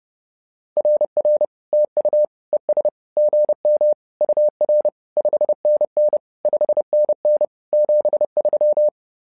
Morse code, heard as a keyed tone, about 30 wpm, 600 Hz. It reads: RR TU ES GM UR 5NN 5NN 73
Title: SOTA QSO Practice 30wpm